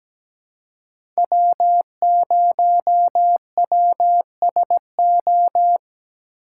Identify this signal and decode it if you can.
Morse code, heard as a keyed tone, about 17 wpm, 700 Hz. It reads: W0WSO
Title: USA Call Signs - 5 Characters - 4x Character Spacing - 17wpm